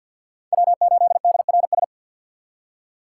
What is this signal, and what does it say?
Morse code, heard as a keyed tone, about 53 wpm, 700 Hz. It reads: W8DRS
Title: USA Call Signs - 5 Characters - 2x Character Spacing - 50wpm